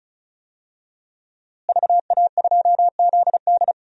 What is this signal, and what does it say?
Morse code, heard as a keyed tone, about 34 wpm, 700 Hz. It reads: VA2ZD
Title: DXCC-100 Call Signs - 5 Characters - Repeated 3x - 35wpm